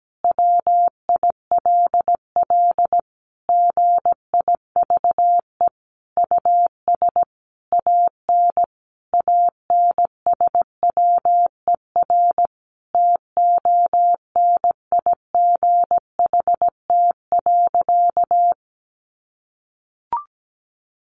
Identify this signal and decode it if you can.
Morse code, heard as a keyed tone, about 17 wpm, 700 Hz. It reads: WILL GIVE US AN ANSWER TONIGHT.
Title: Sentences from Top 600 Words 17wpm